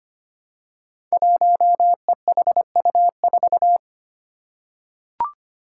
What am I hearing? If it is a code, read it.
Morse code, heard as a keyed tone, about 25 wpm, 700 Hz. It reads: 1EHU4
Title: Five Challenging Letter-Number Combinations 25wpm